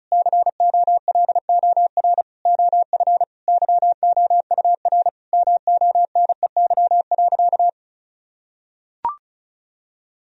Morse code, heard as a keyed tone, about 36 wpm, 700 Hz. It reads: COLOR OF YOUR MONEY.
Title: Sentences from Top 1000 Words - Review Part 2 - 35wpm